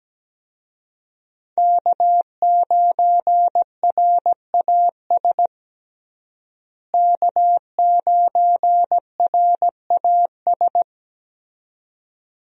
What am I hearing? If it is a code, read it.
Morse code, heard as a keyed tone, about 17 wpm, 700 Hz. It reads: K9RAS K9RAS
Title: USA Call Signs - 5 Characters - Repeated 2x - 17wpm